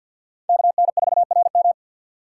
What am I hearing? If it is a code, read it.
Morse code, heard as a keyed tone, about 50 wpm, 700 Hz. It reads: KN4RM
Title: USA Call Signs - 5 Characters - 4x Character Spacing - 50wpm